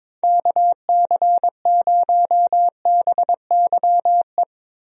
Morse code, heard as a keyed tone, about 22 wpm, 700 Hz. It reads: KC0BYE
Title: USA Call Signs - 6 Characters - Repeated 2x - 22wpm